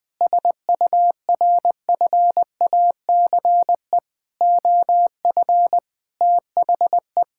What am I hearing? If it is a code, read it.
Morse code, heard as a keyed tone, about 20 wpm, 700 Hz. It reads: SURFACE OF THE
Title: Sentences from Top 1000 Words - Review Part 5 - 20wpm